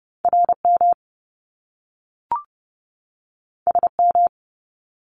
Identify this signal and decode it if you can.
Morse code, heard as a keyed tone, about 30 wpm, 700 Hz. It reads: RM SM
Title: Top Two Letter Combinations in Words 30wpm